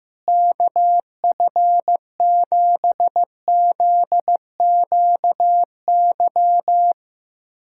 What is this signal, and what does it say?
Morse code, heard as a keyed tone, about 15 wpm, 700 Hz. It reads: KF7ZQY
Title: USA Callsign DE Callsign - 6 Characters - 15wpm